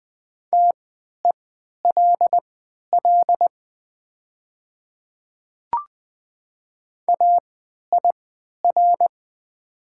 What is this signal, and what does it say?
Morse code, heard as a keyed tone, about 20 wpm, spaced as for 11 wpm, 700 Hz. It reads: TELL AIR
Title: 3x Character Spacing - Top 200 Words - 20wpm